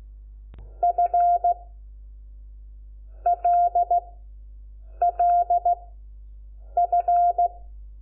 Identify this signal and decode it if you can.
Morse code, heard as a keyed tone, about 16 wpm, 700 Hz. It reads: F L L F
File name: Lecciones / CLASE Nº5